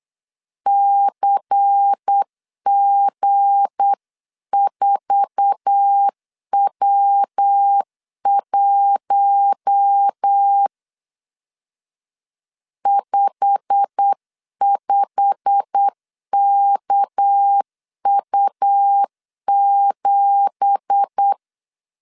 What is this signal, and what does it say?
Morse code, heard as a keyed tone, about 8 wpm, 800 Hz. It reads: CG4W1 55KU7
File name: CW / DET / 1 9 1500C S B 8wpm 5624